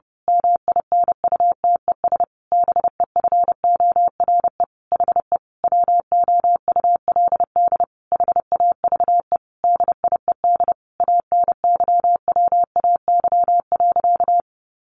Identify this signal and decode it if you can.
Morse code, heard as a keyed tone, about 30 wpm, 700 Hz. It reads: MINUTES BEFORE HE WOULD HAVE DIED ANYWAY.
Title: A Sound Of Thunder 30wpm